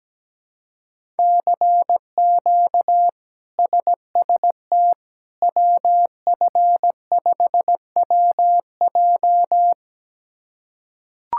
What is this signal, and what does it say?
Morse code, heard as a keyed tone, about 17 wpm, 700 Hz. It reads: CQ SST WF5WJ
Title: K1USN SST QSO Practice - 17wpm